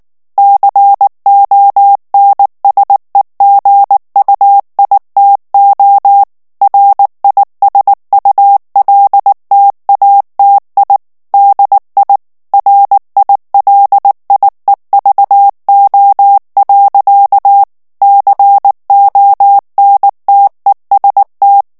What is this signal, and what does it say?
Morse code, heard as a keyed tone, about 19 wpm, 800 Hz. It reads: CONSEGUITO RISULTATI DI RILIEVO. CONTEST
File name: morse mp3 / ITALIEN texte clair 2 vitesse 18 mots minute